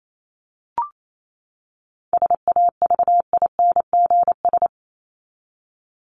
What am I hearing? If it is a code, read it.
Morse code, heard as a keyed tone, about 28 wpm, 700 Hz. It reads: SAVINGS
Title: Top 1001-1250 Words 28wpm